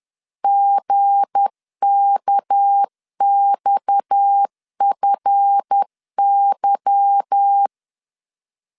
Morse code, heard as a keyed tone, about 10 wpm, 800 Hz. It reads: GKXFY
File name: LEZIONI / 9 2867 10wpm 2733